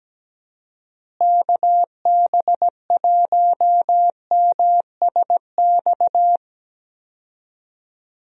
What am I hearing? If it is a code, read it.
Morse code, heard as a keyed tone, about 17 wpm, 700 Hz. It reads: KB1MSX
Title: USA Call Signs - 6 Characters - Repeated 3x - 17wpm